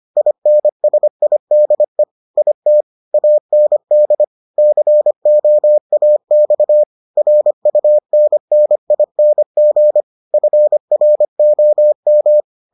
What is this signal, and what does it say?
Morse code, heard as a keyed tone, about 25 wpm, 600 Hz. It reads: INSIDE IT AND COAX RUNNING FROM